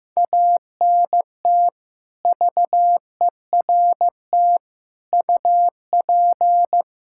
Morse code, heard as a keyed tone, about 16 wpm, 700 Hz. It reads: ANT VERT UP
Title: Practice QSO - Exchange 2 - Info - 15wpm